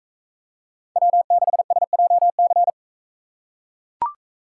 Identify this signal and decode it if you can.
Morse code, heard as a keyed tone, about 38 wpm, 700 Hz. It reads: W6SJC